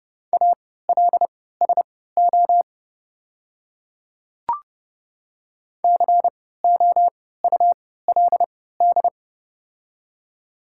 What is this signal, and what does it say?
Morse code, heard as a keyed tone, about 30 wpm, spaced as for 17 wpm, 700 Hz. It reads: ALSO COULD
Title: 3x Character Spacing - Top 200 Words - 30wpm